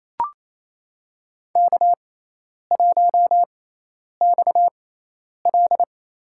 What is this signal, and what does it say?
Morse code, heard as a keyed tone, about 27 wpm, 700 Hz. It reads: K 1 X L
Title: USA Call Signs - 4 Characters - 6x Character Spacing - 28wpm